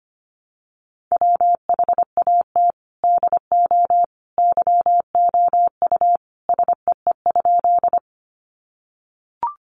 Morse code, heard as a keyed tone, about 25 wpm, 700 Hz. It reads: WHAT DO YOU SEE?
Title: Sentences from Top 100 Words 25wpm